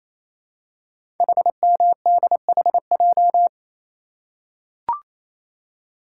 Morse code, heard as a keyed tone, about 29 wpm, 690 Hz. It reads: HMDHJ
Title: Code Groups - 5 Letters - 28wpm